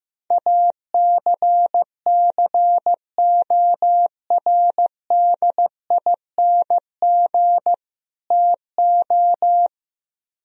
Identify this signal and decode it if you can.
Morse code, heard as a keyed tone, about 15 wpm, 700 Hz. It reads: ACCORDING TO
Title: Speed-Racing - Sets of 2 Words 15wpm